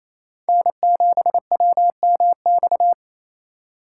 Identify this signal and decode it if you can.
Morse code, heard as a keyed tone, about 35 wpm, 700 Hz. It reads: N7WMX